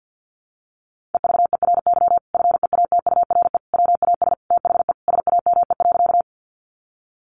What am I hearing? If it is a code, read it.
Morse code, heard as a keyed tone, about 50 wpm, 700 Hz. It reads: EVERY FEATURE WAS THE SAME.
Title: Sentences from Top 800 Words 50wpm